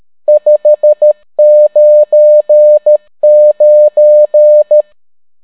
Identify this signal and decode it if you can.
Morse code, heard as a keyed tone, about 13 wpm, 600 Hz. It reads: 599